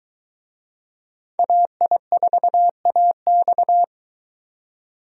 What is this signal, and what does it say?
Morse code, heard as a keyed tone, about 23 wpm, 700 Hz. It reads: AI4AX